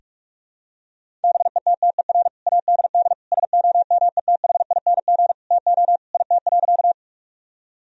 Morse code, heard as a keyed tone, about 46 wpm, 700 Hz. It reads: BETTER ADD SOMETHING TO IT.
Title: Sentences from Top 1000 Words - Review Part 1 - 45wpm